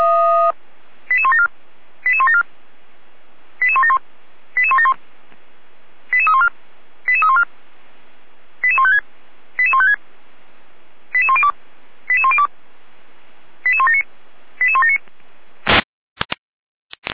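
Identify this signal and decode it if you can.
Five-tone selective call (ZVEI), heard as quick runs of five tones, tones of about 70 ms. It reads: 88174 88174 88171 88171 88314 88314 88156 88156 88102 88102 88179 88179
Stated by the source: ZVEI five-tone call 88164 received on an SDR.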